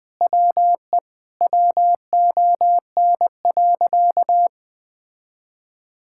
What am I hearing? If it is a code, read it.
Morse code, heard as a keyed tone, about 20 wpm, 700 Hz. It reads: WE WON.
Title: Sentences with 7-Word Phrases - 20wpm